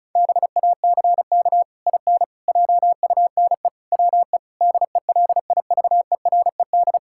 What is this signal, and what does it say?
Morse code, heard as a keyed tone, about 34 wpm, 700 Hz. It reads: BACK IN JUNE WE DELIVERED
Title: Pangram Challenge v2 - 35wpm